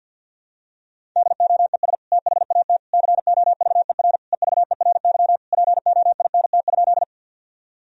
Morse code, heard as a keyed tone, about 49 wpm, 700 Hz. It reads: DOES THAT COVER EVERY POINT?
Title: Sentences from Top 1000 Words - Review Part 1 - 50wpm